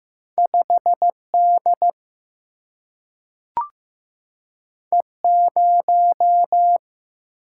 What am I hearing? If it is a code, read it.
Morse code, heard as a keyed tone, about 15 wpm, 700 Hz. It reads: 5 D E 0
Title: Single-Letter Single-Number 15wpm